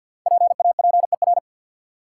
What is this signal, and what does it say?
Morse code, heard as a keyed tone, about 53 wpm, 700 Hz. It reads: PAPER
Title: Five Letter Words - 50wpm